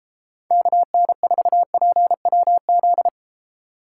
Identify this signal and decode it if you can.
Morse code, heard as a keyed tone, about 34 wpm, 700 Hz. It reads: KN4PWZ
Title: Speed Racing - USA Call Signs 28wpm